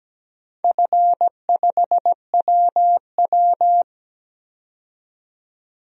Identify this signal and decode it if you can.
Morse code, heard as a keyed tone, about 17 wpm, 700 Hz. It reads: F5WW
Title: DXCC-100 Call Signs - 4 Characters - 17wpm